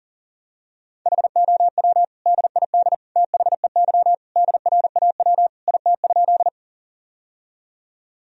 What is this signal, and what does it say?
Morse code, heard as a keyed tone, about 37 wpm, 700 Hz. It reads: HOW DID THEY DRAW IT?
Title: Sentences from Top 1000 Words - Review Part 1 - 40wpm